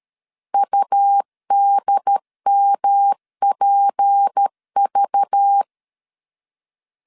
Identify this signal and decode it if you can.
Morse code, heard as a keyed tone, about 13 wpm, 800 Hz. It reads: UDMPV
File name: CW / DET / 1 9 1500C S A 12wpm 3800